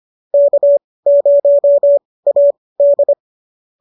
Morse code, heard as a keyed tone, about 25 wpm, spaced as for 18 wpm, 590 Hz. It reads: K0AD